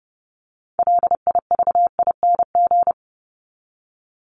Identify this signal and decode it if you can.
Morse code, heard as a keyed tone, about 30 wpm, 700 Hz. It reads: LIVING
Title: ING Suffix - Top 100 Words with Base Words 30wpm